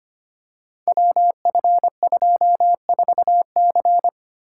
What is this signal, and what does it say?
Morse code, heard as a keyed tone, about 25 wpm, 700 Hz. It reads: WF24C
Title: Code Groups - 5 Letter:Numbers - 25wpm